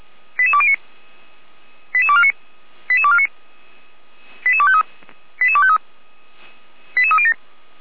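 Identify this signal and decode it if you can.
Five-tone selective call (ZVEI), heard as quick runs of five tones, tones of about 70 ms. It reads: 88209 88249 88249 88253 88253 88307
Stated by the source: ZVEI five-tone selective call received by SDR, from the feuerwehr-orlamuende.org (fire brigade Orlamünde) audio archive.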